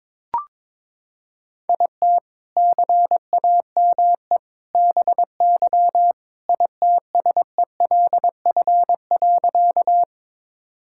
Morse code, heard as a keyed tone, about 22 wpm, 700 Hz. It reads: IT CAME BY ITSELF.
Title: Sentences from Top 1000 Words - Review Part 5 - 22wpm